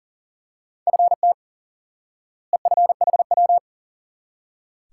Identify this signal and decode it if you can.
Morse code, heard as a keyed tone, about 40 wpm, 700 Hz. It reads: FT EFHW
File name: course / instant-qso-element-course-3 / IQE Course 3 - Lesson 89 - YAGI - Two Elements - 40wpm